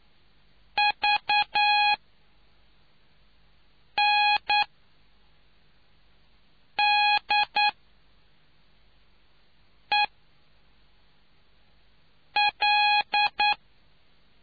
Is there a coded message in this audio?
Yes: Morse code from 0.8 to 13.5 s, heard as a keyed tone, about 9 wpm, spaced as for 3 wpm, 830 Hz. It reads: VNDEL